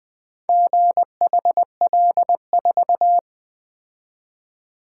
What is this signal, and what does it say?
Morse code, heard as a keyed tone, about 20 wpm, 700 Hz. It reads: GHL4
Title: Four Letter-Number Combinations - 20wpm